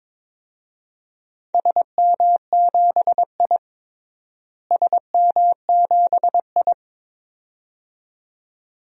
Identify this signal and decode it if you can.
Morse code, heard as a keyed tone, about 21 wpm, 700 Hz. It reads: SM7I SM7I
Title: DXCC-100 Call Signs - 4 Characters - Repeated 2x - 22wpm